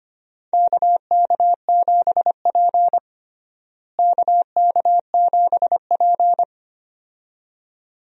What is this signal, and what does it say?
Morse code, heard as a keyed tone, about 25 wpm, 700 Hz. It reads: KK7P KK7P
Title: USA Call Signs - 4 Characters - Repeated 2x - 25wpm